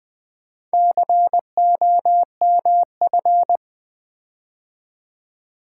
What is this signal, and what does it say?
Morse code, heard as a keyed tone, about 20 wpm, 700 Hz. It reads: COMF
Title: Build A Word - 4 Letters At A Time - 11-Letter Words - 20wpm